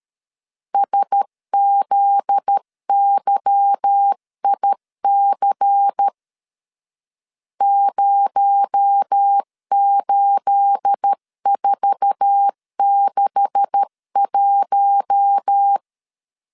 Morse code, heard as a keyed tone, about 13 wpm, 800 Hz. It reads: SZYIC 08461
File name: LEZIONI / 9 286 12wpm 2305